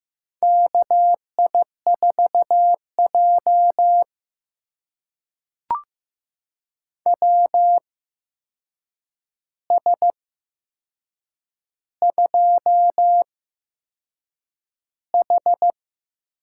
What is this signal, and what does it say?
Morse code, heard as a keyed tone, about 15 wpm, spaced as for 6 wpm, 700 Hz. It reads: KI4J WS2H
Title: USA Call Signs - 4 Characters - 8x Character Spacing - 15wpm